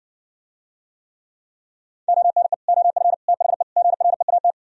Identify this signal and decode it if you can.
Morse code, heard as a keyed tone, about 57 wpm, 700 Hz. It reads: ONE OF THE GREAT